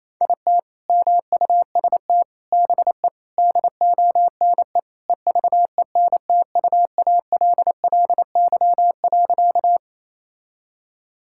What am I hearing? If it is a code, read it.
Morse code, heard as a keyed tone, about 28 wpm, 700 Hz. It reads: IT MUST BE DONE EVENTUALLY.
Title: Sentences from Top 1000 Words - Review Part 6 - 28wpm